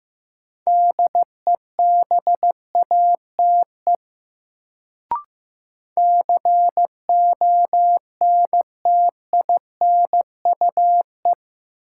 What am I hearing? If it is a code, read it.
Morse code, heard as a keyed tone, about 15 wpm, 700 Hz. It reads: DEBATE CONTINUE
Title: Sentences from Top 1100 Words - New Word Drill Alt - 15wpm